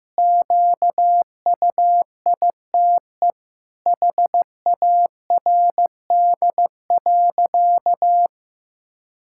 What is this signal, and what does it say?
Morse code, heard as a keyed tone, about 15 wpm, 700 Hz. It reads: QUITE HARD.
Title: Pangram Challenge v2 - 15wpm